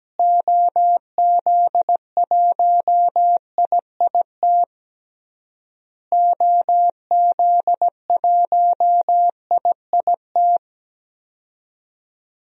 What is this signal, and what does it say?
Morse code, heard as a keyed tone, about 17 wpm, 700 Hz. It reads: OZ1IIT OZ1IIT
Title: DXCC-100 Call Signs - 6 Characters - Repeated 3x - 17wpm